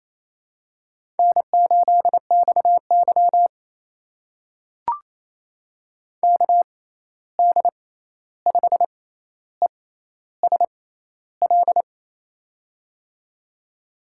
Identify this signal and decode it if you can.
Morse code, heard as a keyed tone, about 28 wpm, spaced as for 13 wpm, 700 Hz. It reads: N8XY KD5ESL
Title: USA Call Signs - 6x Character Spacing - 28wpm